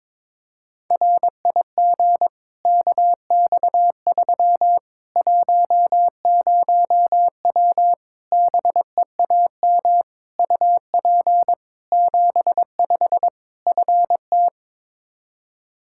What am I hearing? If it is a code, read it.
Morse code, heard as a keyed tone, about 22 wpm, 700 Hz. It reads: RIG KX3 10W BEAM UP 75 FT
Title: Practice QSO - Exchange 2 - Info - 22wpm